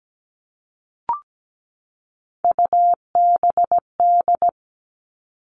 Morse code, heard as a keyed tone, about 17 wpm, 700 Hz. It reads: UBD
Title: All Cut Numbers - 3-Digit 17wpm